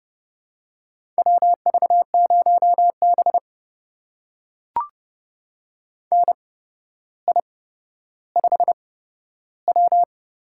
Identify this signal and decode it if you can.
Morse code, heard as a keyed tone, about 31 wpm, spaced as for 13 wpm, 700 Hz. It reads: WV0B NI5W
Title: USA Call Signs - 8x Character Spacing - 30wpm